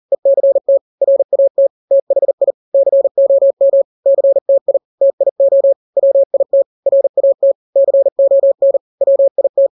Morse code, heard as a keyed tone, about 40 wpm, 550 Hz. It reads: ECT RAT THI COM CTI TIO WIT RAT CON WIT
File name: n-grams / 3-grams 40 35 WPM